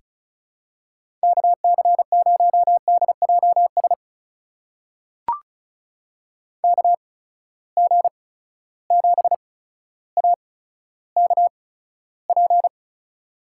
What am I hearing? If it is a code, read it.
Morse code, heard as a keyed tone, about 36 wpm, spaced as for 15 wpm, 700 Hz. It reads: KC0DJS KG7AKP
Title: USA Call Signs - 6 Characters - 8x Character Spacing - 35wpm